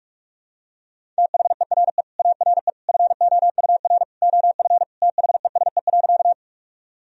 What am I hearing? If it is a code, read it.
Morse code, heard as a keyed tone, about 41 wpm, 700 Hz. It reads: THERE ARE FOUR OF THESE.
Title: Sentences from Top 1000 Words - Review Part 1 - 45wpm